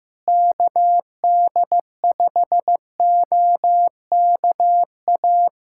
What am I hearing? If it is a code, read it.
Morse code, heard as a keyed tone, about 15 wpm, 700 Hz. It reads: KD5OKA
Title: USA Callsign DE Callsign - 6 Characters - 15wpm